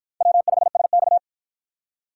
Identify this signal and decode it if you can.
Morse code, heard as a keyed tone, about 50 wpm, 700 Hz. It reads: W5SX